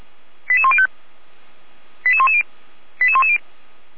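Five-tone selective call (ZVEI), heard as quick runs of five tones, tones of about 70 ms. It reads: 88105 88110 88110
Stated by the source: ZVEI five-tone call received by SDR, recorded by Feuerwehr Orlamünde (f85360).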